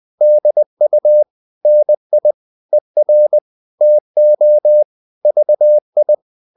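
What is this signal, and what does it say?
Morse code, heard as a keyed tone, about 20 wpm, 600 Hz. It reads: DU NI ER TO VI
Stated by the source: Morse code practice of short Danish words at 20 wpm, from CW med Gnister.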